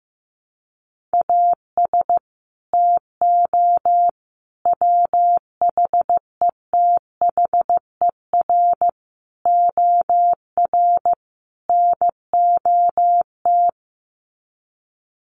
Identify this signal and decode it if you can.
Morse code, heard as a keyed tone, about 15 wpm, 700 Hz. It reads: AS TO WHETHER OR NOT
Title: Sets of 5 Words Encore 15wpm